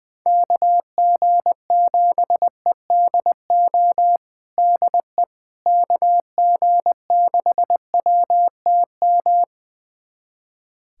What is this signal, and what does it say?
Morse code, heard as a keyed tone, about 20 wpm, 700 Hz. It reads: KG7EDO DE KG6WTM
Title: USA Callsign DE Callsign 20wpm